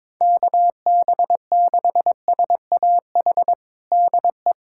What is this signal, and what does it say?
Morse code, heard as a keyed tone, about 22 wpm, 700 Hz. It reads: KB6SAH DE
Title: USA Callsign DE Callsign 22wpm